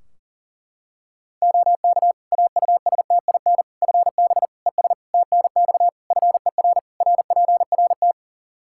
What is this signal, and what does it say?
Morse code, heard as a keyed tone, about 37 wpm, 700 Hz. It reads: OK AUSTIN FB ES TNX FER RPRT
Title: Practice QSO - Exchange 2 - Info - 40wpm